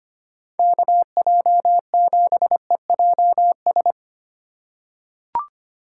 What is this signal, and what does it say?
Morse code, heard as a keyed tone, about 25 wpm, 700 Hz. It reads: KJ7EJS
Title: USA Call Signs - 6 Characters - 4x Character Spacing - 25wpm